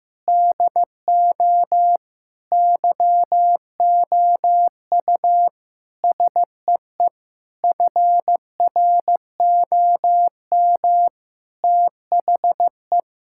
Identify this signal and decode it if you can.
Morse code, heard as a keyed tone, about 16 wpm, 700 Hz. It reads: DO YOU SEE FROM THE
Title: Sentences from Top 1600 Words 15wpm